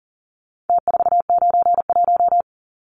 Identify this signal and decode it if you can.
Morse code, heard as a keyed tone, about 38 wpm, 700 Hz. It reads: T491
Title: Numbers - Contest Style - 4-Digit 40wpm